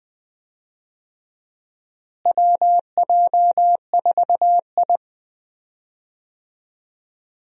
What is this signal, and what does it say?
Morse code, heard as a keyed tone, about 21 wpm, 700 Hz. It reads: WJ4I
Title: USA Call Signs - 4 Characters - Repeated 3x - 20wpm